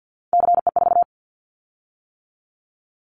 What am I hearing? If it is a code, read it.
Morse code, heard as a keyed tone, about 50 wpm, 700 Hz. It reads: CE4
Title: Callsign Prefixes for Top 100 DXCC - 50wpm